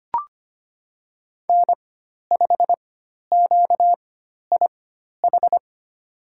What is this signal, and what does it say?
Morse code, heard as a keyed tone, about 25 wpm, spaced as for 12 wpm, 700 Hz. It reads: N5QIH
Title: USA Call Signs - 5 Characters - 4x Character Spacing - 25wpm